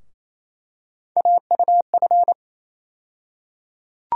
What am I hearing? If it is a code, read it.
Morse code, heard as a keyed tone, about 28 wpm, 700 Hz. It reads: AUF
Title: USA Call Sign Postfixes - 3 Letters 28wpm